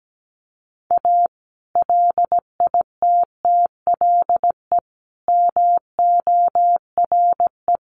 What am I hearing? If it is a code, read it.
Morse code, heard as a keyed tone, about 17 wpm, 700 Hz. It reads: A LITTLE MORE
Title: Sets of 5 Words Encore2 17wpm